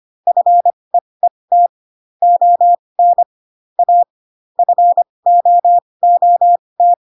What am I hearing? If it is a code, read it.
Morse code, heard as a keyed tone, about 25 wpm, spaced as for 20 wpm, 700 Hz. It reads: FEET ON A FOOT
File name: Pinocchio / Pinocchio ch6 15 25 ik1hge